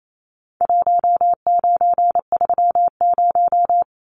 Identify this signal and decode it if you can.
Morse code, heard as a keyed tone, about 28 wpm, 700 Hz. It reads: 1930
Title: Numbers - Contest Style - 4-Digit 28wpm